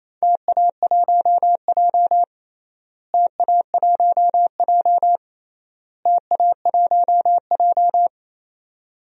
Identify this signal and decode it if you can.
Morse code, heard as a keyed tone, about 28 wpm, 700 Hz. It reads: TA1J TA1J TA1J
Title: DXCC-100 Call Signs - 4 Characters - Repeated 3x - 28wpm